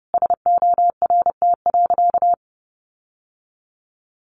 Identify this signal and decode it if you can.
Morse code, heard as a keyed tone, about 30 wpm, 700 Hz. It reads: SORT.
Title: Sentences from Top 1300 Words 30wpm